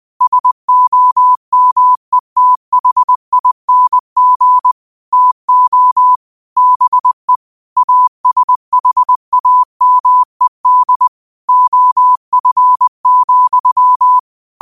Morse code, heard as a keyed tone, about 20 wpm, 1000 Hz. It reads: SOMETHING TO BE ASHAMED OF,
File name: qotdfiles / 20wpm